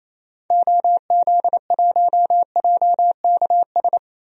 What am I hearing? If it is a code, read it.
Morse code, heard as a keyed tone, about 28 wpm, 700 Hz. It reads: OZ1JKS
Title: DXCC-100 Call Signs - 6 Characters - Repeated 2x - 28wpm